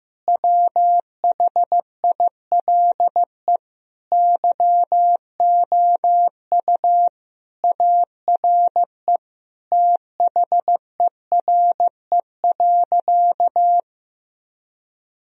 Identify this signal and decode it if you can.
Morse code, heard as a keyed tone, about 15 wpm, 700 Hz. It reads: WHILE YOU ARE THERE.
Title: Sentences from Top 1000 Words - Review Part 1 - 15wpm